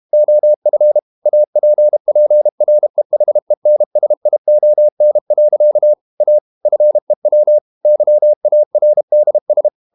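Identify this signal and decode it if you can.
Morse code, heard as a keyed tone, about 32 wpm, 600 Hz. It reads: OF APPREHENSION. A FEW YARDS
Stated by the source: Morse code practice at 32 wpm from CW med Gnister, reading Ken Follett's novel Never in English.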